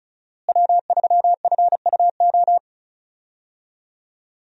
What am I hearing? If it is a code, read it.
Morse code, heard as a keyed tone, about 34 wpm, 700 Hz. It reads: W3FUO
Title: USA Call Signs - 5 Characters - Repeated 3x - 35wpm